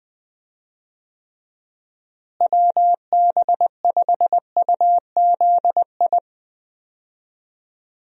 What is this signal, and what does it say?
Morse code, heard as a keyed tone, about 19 wpm, 700 Hz. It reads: WB5UZI
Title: USA Call Signs - 6 Characters - Repeated 3x - 20wpm